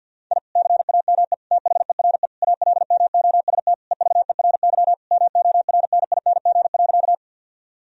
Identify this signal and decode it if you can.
Morse code, heard as a keyed tone, about 50 wpm, 700 Hz. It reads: I CAME THERE ALMOST EVERY MORNING.
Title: Sentences from Top 1000 Words - Review Part 4 - 50wpm